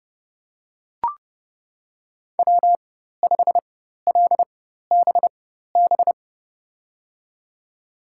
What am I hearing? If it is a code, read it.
Morse code, heard as a keyed tone, about 29 wpm, spaced as for 14 wpm, 700 Hz. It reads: W5LBB
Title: USA Call Signs - 4x Character Spacing - 30wpm